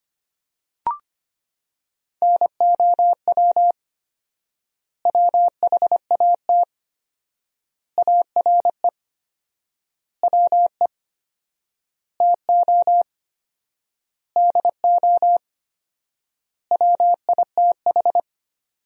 Morse code, heard as a keyed tone, about 25 wpm, 700 Hz. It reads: NOW WHAT ARE WE TO DO WITH
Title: 4x Word-Spacing - Sentences from Top 100 Words - 25wpm